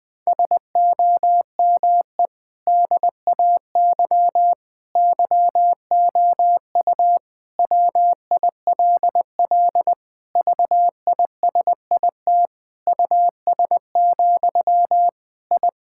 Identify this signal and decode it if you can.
Morse code, heard as a keyed tone, about 21 wpm, 700 Hz. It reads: SOME DAY YOU WILL VISIT US, I
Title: Sentences from Top 1000 Words - Review Part 6 - 20wpm